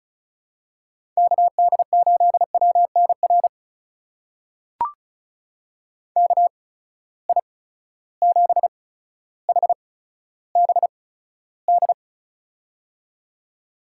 Morse code, heard as a keyed tone, about 36 wpm, spaced as for 15 wpm, 700 Hz. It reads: KD8WNR KI7HBD
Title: USA Call Signs - 6 Characters - 8x Character Spacing - 35wpm